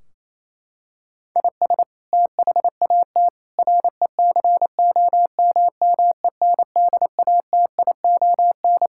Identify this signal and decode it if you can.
Morse code, heard as a keyed tone, about 28 wpm, 700 Hz. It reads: IS THAT RECOMMENDATION
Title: Sentences from Top 1600 Words 28wpm